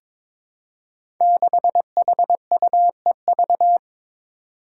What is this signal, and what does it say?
Morse code, heard as a keyed tone, about 22 wpm, 700 Hz. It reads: 6HUEV
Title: Five Challenging Letter-Number Combinations 22wpm